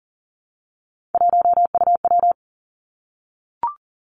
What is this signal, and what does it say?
Morse code, heard as a keyed tone, about 39 wpm, 700 Hz. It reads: 1UW
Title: Three Challenging Letter-Number Combinations 40wpm